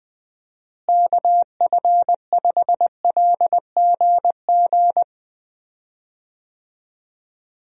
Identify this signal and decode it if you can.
Morse code, heard as a keyed tone, about 20 wpm, 700 Hz. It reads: KF5LGG
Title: USA Call Signs - 6 Characters - Repeated 2x - 20wpm